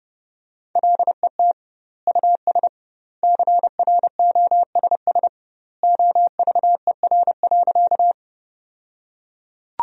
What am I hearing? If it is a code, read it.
Morse code, heard as a keyed tone, about 30 wpm, 700 Hz. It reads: LET US CROSS OVER.
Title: 2x Word-Spacing - Sentences from Top 300 Words - 30wpm